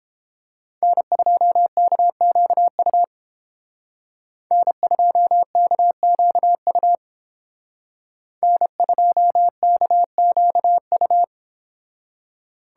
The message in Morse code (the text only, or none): N2KQU N2KQU N2KQU